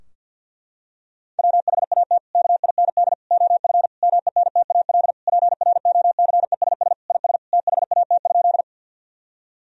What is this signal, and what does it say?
Morse code, heard as a keyed tone, about 46 wpm, 700 Hz. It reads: WHAT KIND OF MENTAL PROCESS IS THAT?
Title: Sentences from Top 1000 Words - Review Part 4 - 50wpm